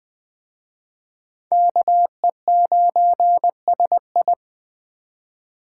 Morse code, heard as a keyed tone, about 20 wpm, 700 Hz. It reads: KE9SI
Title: Speed Racing - USA 2x2 Call Signs 17wpm